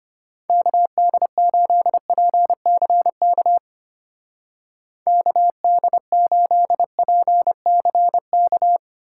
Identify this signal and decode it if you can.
Morse code, heard as a keyed tone, about 31 wpm, 700 Hz. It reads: KD8PCK KD8PCK
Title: Speed Racing - USA 2x3 Call Signs 25wpm